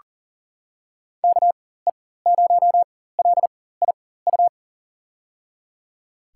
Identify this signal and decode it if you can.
Morse code, heard as a keyed tone, about 40 wpm, spaced as for 19 wpm, 700 Hz. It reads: KE0LIU
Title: USA Call Signs - 6 Characters - 4x Character Spacing - 40wpm